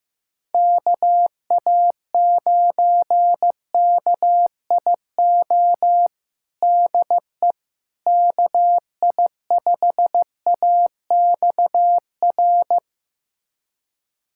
Morse code, heard as a keyed tone, about 15 wpm, 700 Hz. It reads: KA9KIO DE KI5AXR
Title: USA Callsign DE Callsign - 6 Characters - 15wpm